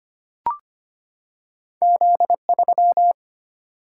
Morse code, heard as a keyed tone, about 25 wpm, 700 Hz. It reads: Z3
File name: Two Letter-Number Combinations - 25wpm